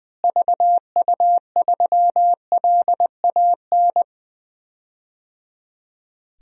Morse code, heard as a keyed tone, about 20 wpm, 700 Hz. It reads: VU3LAN
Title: DXCC-100 Call Signs - 6 Characters - Repeated 3x - 20wpm